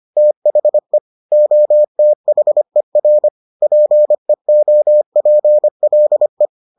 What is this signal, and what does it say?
Morse code, heard as a keyed tone, about 25 wpm, 600 Hz. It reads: THE OTHER PEOPLE